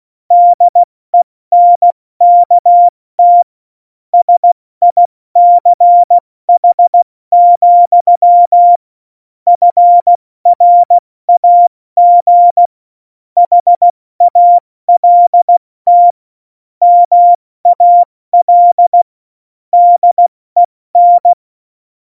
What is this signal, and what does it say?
Morse code, heard as a keyed tone, about 16 wpm, 700 Hz. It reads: DENKT SICH, FRAG HALT MAL DEN
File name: Morsekurs / Witze / Witz 29. Januar 2019, 70 F80